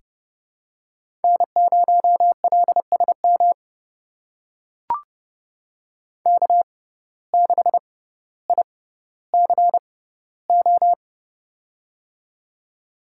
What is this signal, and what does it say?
Morse code, heard as a keyed tone, about 30 wpm, spaced as for 15 wpm, 700 Hz. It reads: N0LSM K6ICO
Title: USA Call Signs - 6x Character Spacing - 30wpm